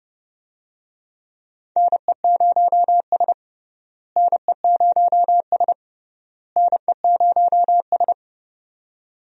Morse code, heard as a keyed tone, about 30 wpm, 700 Hz. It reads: NE0S NE0S NE0S
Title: USA Call Signs - 4 Characters - Repeated 3x - 30wpm